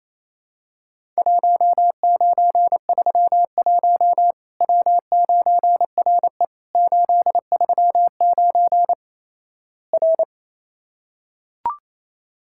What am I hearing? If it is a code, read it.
Morse code, heard as a keyed tone, about 28 wpm, 700 Hz. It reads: 1931 W9RE 839 R
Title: Worked All Europe DX Contest - QTC Exchange - 28wpm